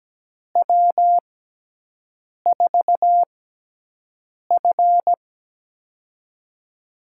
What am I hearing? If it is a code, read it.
Morse code, heard as a keyed tone, about 17 wpm, spaced as for 6 wpm, 700 Hz. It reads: W4F
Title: USA Call Signs - 4 Characters - 6x Character Spacing - 17wpm